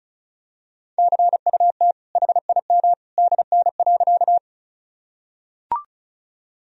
Morse code, heard as a keyed tone, about 34 wpm, 700 Hz. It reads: CUT HIM DN.
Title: Sentences from Top 300 Words and Abbreviations 35wpm